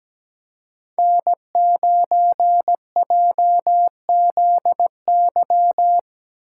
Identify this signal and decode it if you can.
Morse code, heard as a keyed tone, about 17 wpm, 700 Hz. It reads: N9JZY
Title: K1USN SST QSO Practice - 17wpm